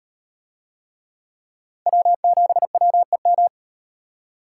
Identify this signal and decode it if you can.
Morse code, heard as a keyed tone, about 38 wpm, 700 Hz. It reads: W7WEM